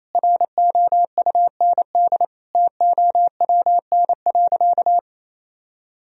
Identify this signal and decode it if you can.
Morse code, heard as a keyed tone, about 28 wpm, 700 Hz. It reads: ROUND TOWN.
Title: Sentences from Top 1000 Words - Review Part 1 - 28wpm